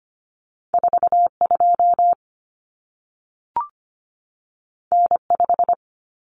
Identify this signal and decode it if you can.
Morse code, heard as a keyed tone, about 25 wpm, 700 Hz. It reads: 42 N5
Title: Common Cut Numbers - 2-Digit 25wpm